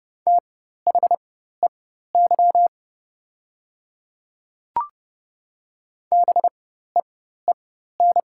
Morse code, heard as a keyed tone, about 30 wpm, spaced as for 14 wpm, 700 Hz. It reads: THEY BEEN
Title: 4x Character Spacing - Top 100 Words - 30wpm